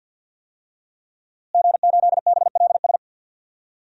Morse code, heard as a keyed tone, about 50 wpm, 700 Hz. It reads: G8BLS
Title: DXCC-100 Call Signs - 5 Characters - Repeated 2x - 50wpm